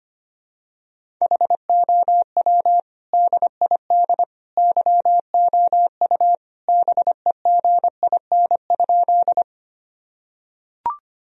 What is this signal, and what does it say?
Morse code, heard as a keyed tone, about 25 wpm, 700 Hz. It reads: HOW DID YOU BEGIN?
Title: Sentences from Top 1000 Words - Review Part 1 - 25wpm